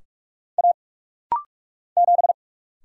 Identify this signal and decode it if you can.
Morse code, heard as a keyed tone, about 46 wpm, 700 Hz. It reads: A 7
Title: Single Letter-Number Alt - Rapid-Fire 45wpm